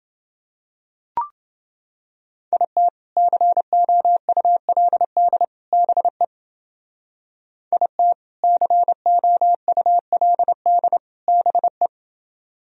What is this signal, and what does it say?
Morse code, heard as a keyed tone, about 28 wpm, 700 Hz. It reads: IT COULD BE IT COULD BE